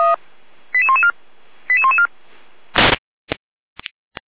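Five-tone selective call (ZVEI), heard as quick runs of five tones, tones of about 80 ms. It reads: 88114 88114
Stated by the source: ZVEI five-tone selective call 88113 received by SDR.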